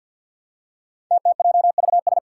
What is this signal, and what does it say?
Morse code, heard as a keyed tone, about 47 wpm, 700 Hz. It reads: TTJVS
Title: Five Challenging Letter-Number Combinations 50wpm